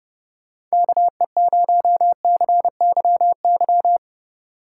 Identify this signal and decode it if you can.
Morse code, heard as a keyed tone, about 36 wpm, 700 Hz. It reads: KE0CYY
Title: Speed Racing - USA 2x3 Call Signs 30wpm